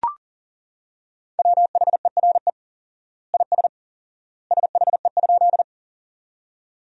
Morse code, heard as a keyed tone, about 40 wpm, 700 Hz. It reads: WHERE IS SHE?
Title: 4x Word-Spacing - Sentences from Top 200 Words - 40wpm